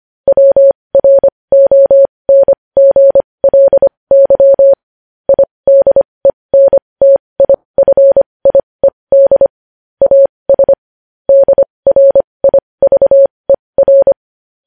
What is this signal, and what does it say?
Morse code, heard as a keyed tone, about 25 wpm, spaced as for 20 wpm, 550 Hz. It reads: WRONGLY IDENTIFIED AS DRIVER
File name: podcast / News headlines-1x-25x20